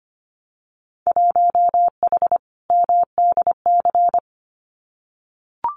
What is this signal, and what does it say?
Morse code, heard as a keyed tone, about 25 wpm, 690 Hz. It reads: 1H MDC
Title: Winter Field Day Exchanges 25wpm